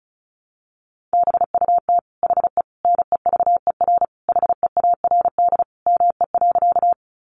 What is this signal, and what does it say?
Morse code, heard as a keyed tone, about 34 wpm, 700 Hz. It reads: BUT HE NEVER HEARD ME.
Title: Sentences from Top 400 Words 35wpm